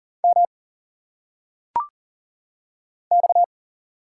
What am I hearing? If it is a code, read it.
Morse code, heard as a keyed tone, about 40 wpm, 700 Hz. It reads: M X
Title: Single Letters - Phonetic Pronunciation 40wpm